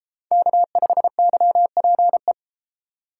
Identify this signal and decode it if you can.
Morse code, heard as a keyed tone, about 32 wpm, 700 Hz. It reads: K5YPE